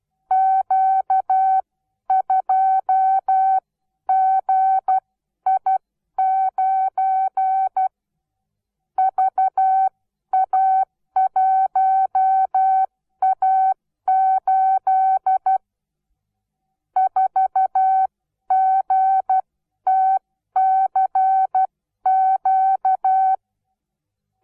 Morse code, heard as a keyed tone, about 12 wpm, spaced as for 10 wpm, 760 Hz. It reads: Q2GI9 VA1A8 4GTCQ
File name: Onlinelehrgang / morsen / DJ4UF Morsen17